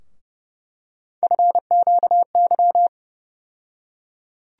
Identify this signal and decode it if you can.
Morse code, heard as a keyed tone, about 29 wpm, 700 Hz. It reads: FQY
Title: USA Call Sign Postfixes - 3 Letters 30wpm